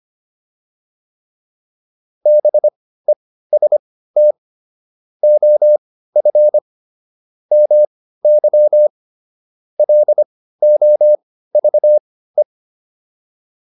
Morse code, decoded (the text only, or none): BEST OF MY LOVE